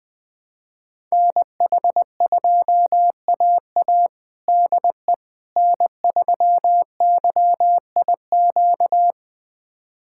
Morse code, heard as a keyed tone, about 21 wpm, 700 Hz. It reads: NH2AA DE N3YIQ
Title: USA Callsign DE Callsign - 5 Characters - 20wpm